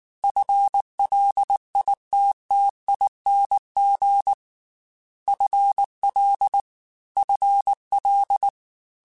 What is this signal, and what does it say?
Morse code, heard as a keyed tone, about 19 wpm, 800 Hz. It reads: FLITTING FL FL
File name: FL / FL mp3s / FL Prefixes 3x2 @18